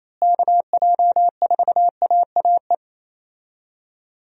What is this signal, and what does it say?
Morse code, heard as a keyed tone, about 28 wpm, 700 Hz. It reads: KJ4AAE
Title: USA Call Signs - 6 Characters - Repeated 2x - 28wpm